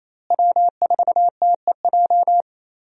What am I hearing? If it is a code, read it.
Morse code, heard as a keyed tone, about 28 wpm, 700 Hz. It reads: W4TEJ
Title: USA Call Signs - 5 Characters - Repeated 3x - 28wpm